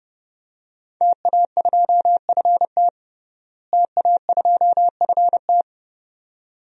Morse code, heard as a keyed tone, about 32 wpm, 700 Hz. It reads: TA2FT TA2FT
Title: DXCC-100 Call Signs - 5 Characters - Repeated 2x - 30wpm